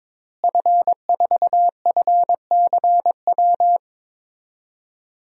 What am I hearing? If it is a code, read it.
Morse code, heard as a keyed tone, about 22 wpm, 700 Hz. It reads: F4FCW
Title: DXCC-100 Call Signs - 5 Characters - Repeated 3x - 22wpm